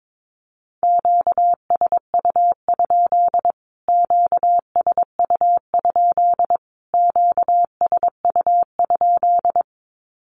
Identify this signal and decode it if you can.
Morse code, heard as a keyed tone, about 22 wpm, 700 Hz. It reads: QSU? QSU? QSU?
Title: Q-Codes Repeated 3x 22wpm